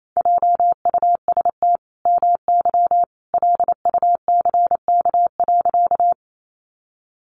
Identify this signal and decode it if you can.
Morse code, heard as a keyed tone, about 28 wpm, 700 Hz. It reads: JUST MY LUCK.
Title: Sentences from Top 1500 Words 28wpm